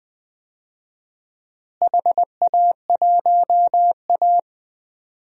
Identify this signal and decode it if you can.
Morse code, heard as a keyed tone, about 20 wpm, 700 Hz. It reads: HA1A
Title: DXCC-100 Call Signs - 4 Characters - Repeated 2x - 20wpm